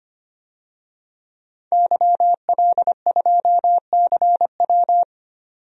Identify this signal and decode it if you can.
Morse code, heard as a keyed tone, about 26 wpm, 690 Hz. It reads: YL2CW
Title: DXCC-100 Call Signs - 5 Characters - 25wpm